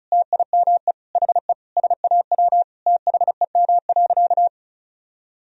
Morse code, heard as a keyed tone, about 36 wpm, 700 Hz. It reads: TIME HE SAW THEM.
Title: Sentences from Top 1000 Words - Review Part 1 - 35wpm